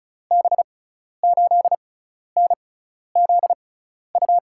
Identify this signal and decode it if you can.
Morse code, heard as a keyed tone, about 36 wpm, spaced as for 12 wpm, 700 Hz. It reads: B8NZU
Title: USA Call Signs - 6 Characters - 6x Character Spacing - 35wpm